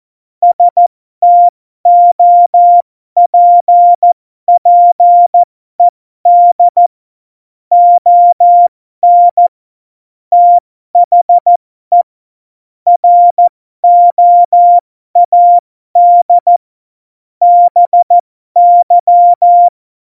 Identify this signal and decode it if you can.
Morse code, heard as a keyed tone, about 14 wpm, spaced as for 12 wpm, 700 Hz. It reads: STOPPED ON THE ROAD BY
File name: Morsekurs / Witze